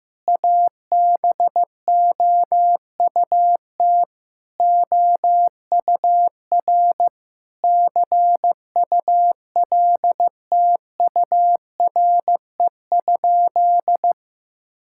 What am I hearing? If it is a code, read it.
Morse code, heard as a keyed tone, about 15 wpm, 700 Hz. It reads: ABOUT OUR CULTURE?
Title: Sentences from Top 1000 Words - Review Part 6 - 15wpm